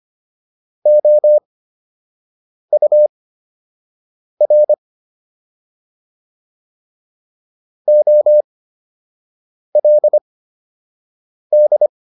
Morse code, heard as a keyed tone, about 25 wpm, spaced as for 6 wpm, 600 Hz. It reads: OUR OLD